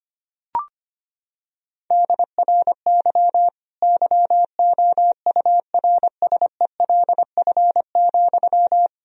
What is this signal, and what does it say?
Morse code, heard as a keyed tone, about 25 wpm, 700 Hz. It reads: DRY YOURSELF,
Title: Sentences from Top 1000 Words - Review Part 3 - 25wpm